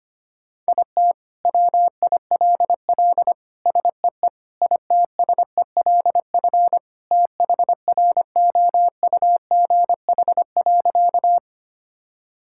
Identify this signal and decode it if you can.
Morse code, heard as a keyed tone, about 25 wpm, 700 Hz. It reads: IT WILL SEE ITSELF THROUGH.
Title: Sentences from Top 1000 Words - Review Part 5 - 25wpm